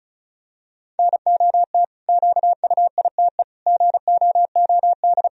Morse code, heard as a keyed tone, about 36 wpm, 700 Hz. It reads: NOT QUITE GOOD
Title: Sentences from Top 1000 Words - Review Part 1 - 35wpm